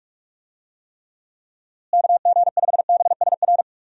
Morse code, heard as a keyed tone, about 46 wpm, 700 Hz. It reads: KG5BSR
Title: USA Call Signs - 6 Characters - Repeated 3x - 45wpm